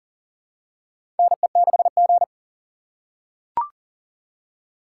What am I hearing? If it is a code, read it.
Morse code, heard as a keyed tone, about 39 wpm, 700 Hz. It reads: NE6G
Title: USA Call Signs - 4 Characters - 3x Character Spacing - 40wpm